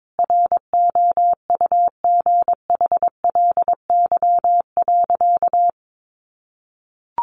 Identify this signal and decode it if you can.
Morse code, heard as a keyed tone, about 22 wpm, 700 Hz. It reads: ROUGHLY.
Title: Sentences from Top 1500 Words 22wpm